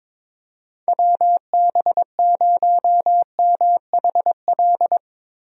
Morse code, heard as a keyed tone, about 22 wpm, 700 Hz. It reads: WB0MHL
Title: USA Call Signs - 6 Characters - Repeated 2x - 22wpm